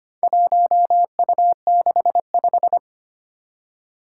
Morse code, heard as a keyed tone, about 25 wpm, 700 Hz. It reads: 1U65
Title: Four Challenging Letter-Number Combinations 25wpm